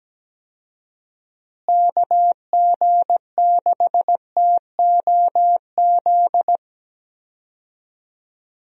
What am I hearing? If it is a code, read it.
Morse code, heard as a keyed tone, about 17 wpm, 700 Hz. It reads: KG6TOZ
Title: USA Call Signs - 6 Characters - Repeated 2x - 17wpm